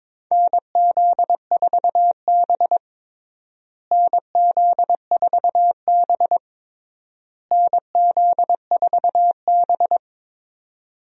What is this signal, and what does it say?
Morse code, heard as a keyed tone, about 22 wpm, 700 Hz. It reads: NZ4B NZ4B NZ4B
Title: USA Call Signs - 4 Characters - Repeated 3x - 22wpm